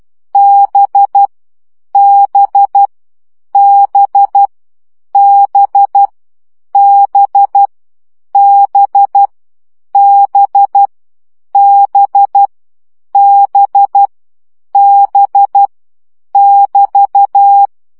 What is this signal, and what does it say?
Morse code, heard as a keyed tone, about 12 wpm, 800 Hz. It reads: B B B B B B B B B B <BT>
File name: CW gyakorlas b 10 baud